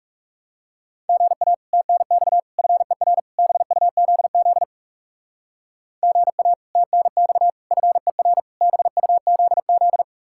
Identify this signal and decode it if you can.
Morse code, heard as a keyed tone, about 36 wpm, 700 Hz. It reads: GA TNX FER BUZZ GA TNX FER BUZZ
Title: Speed-Racing Common QSO Phrases - 30wpm